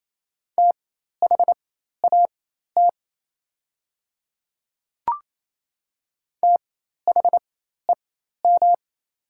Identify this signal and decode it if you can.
Morse code, heard as a keyed tone, about 27 wpm, spaced as for 13 wpm, 700 Hz. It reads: THAT THEM
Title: 4x Character Spacing - Top 100 Words - 28wpm